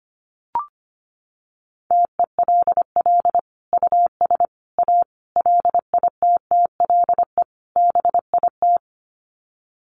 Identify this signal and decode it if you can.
Morse code, heard as a keyed tone, about 25 wpm, 700 Hz. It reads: TELL US A LITTLE BIT
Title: Sets of 5 Words Encore2 25wpm